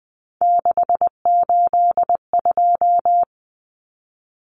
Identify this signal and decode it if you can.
Morse code, heard as a keyed tone, about 20 wpm, 700 Hz. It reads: 682
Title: Common Cut Numbers - 3-Digit 20wpm